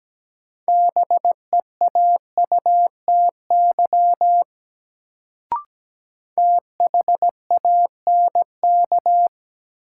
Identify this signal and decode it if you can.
Morse code, heard as a keyed tone, about 17 wpm, 700 Hz. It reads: BEAUTY THANK
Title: Sentences from Top 600 Words - New Word Drill - 17wpm